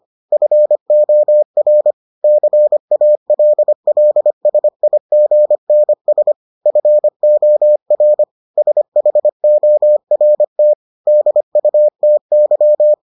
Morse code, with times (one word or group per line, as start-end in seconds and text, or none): 0.3-1.9 FOR
2.2-6.3 CALLSIGNS
6.7-8.2 FOR
8.6-10.7 SHORT
11.1-12.9 DUTY